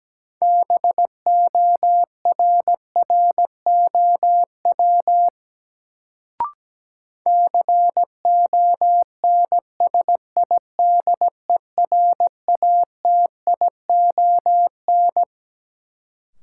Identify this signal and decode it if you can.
Morse code, heard as a keyed tone, about 17 wpm, 700 Hz. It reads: BORROW CONSIDERATION
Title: Sentences from Top 1500 Words - New Word Drill - 17wpm